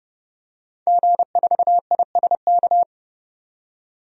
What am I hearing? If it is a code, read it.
Morse code, heard as a keyed tone, about 30 wpm, 700 Hz. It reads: G4ISK
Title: DXCC-100 Call Signs - 5 Characters - 30wpm